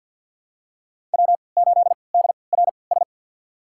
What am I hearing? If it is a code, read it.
Morse code, heard as a keyed tone, about 52 wpm, spaced as for 29 wpm, 700 Hz. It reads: W8DRS
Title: USA Call Signs - 5 Characters - 3x Character Spacing - 50wpm